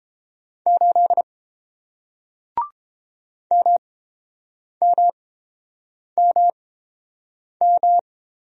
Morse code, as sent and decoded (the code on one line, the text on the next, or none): ---.. / -- -- -- --
8 MMMM